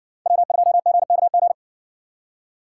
Morse code, heard as a keyed tone, about 59 wpm, 700 Hz. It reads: W2GKG